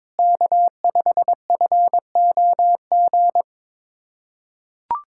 Morse code, heard as a keyed tone, about 22 wpm, 700 Hz. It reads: K5FOG
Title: USA Callsign DE Callsign - 5 Characters - 22wpm